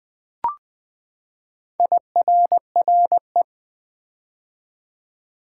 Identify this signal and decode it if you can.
Morse code, heard as a keyed tone, about 20 wpm, 700 Hz. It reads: IRRE
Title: Build A Word - 4 Letters At A Time - 10-Letter Words - 20wpm